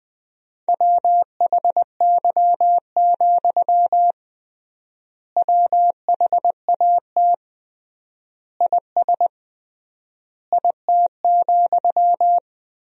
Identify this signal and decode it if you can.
Morse code, heard as a keyed tone, about 21 wpm, 700 Hz. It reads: WHY, WHAT IS IT,
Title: 3x Word-Spacing - Sentences from Top 200 Words - 20wpm